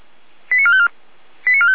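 Five-tone selective call (ZVEI), heard as quick runs of five tones, tones of about 80 ms. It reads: 89454
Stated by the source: ZVEI five-tone selective call received on an SDR.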